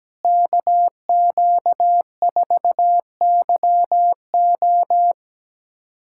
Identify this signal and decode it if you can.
Morse code, heard as a keyed tone, about 17 wpm, 700 Hz. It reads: KQ4YO
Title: USA Callsign DE Callsign - 5 Characters - 17wpm